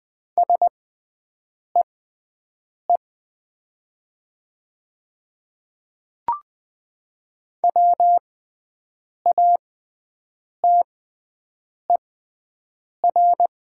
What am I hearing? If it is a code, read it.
Morse code, heard as a keyed tone, about 20 wpm, spaced as for 7 wpm, 700 Hz. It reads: SEE WATER
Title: 6x Character Spacing - Top 100 Words - 20wpm